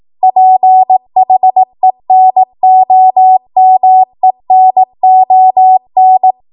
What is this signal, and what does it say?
Morse code, heard as a keyed tone, about 18 wpm, 750 Hz. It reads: PHENOMENON